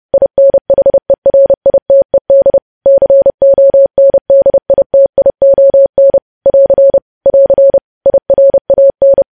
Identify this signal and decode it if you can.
Morse code, heard as a keyed tone, about 30 wpm, 550 Hz. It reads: INHERITED CONDITION <AR> <AR> IRAN
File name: podcast / News headlines-1x-30x30